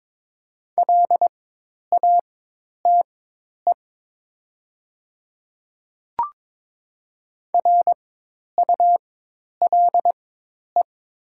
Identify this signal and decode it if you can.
Morse code, heard as a keyed tone, about 22 wpm, spaced as for 10 wpm, 700 Hz. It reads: LATE RULE
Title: 4x Character Spacing - Top 500 Words - 22wpm